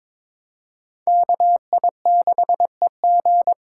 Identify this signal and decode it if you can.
Morse code, heard as a keyed tone, about 22 wpm, 700 Hz. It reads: KI6EG
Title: USA Call Signs - 5 Characters - 4x Character Spacing - 22wpm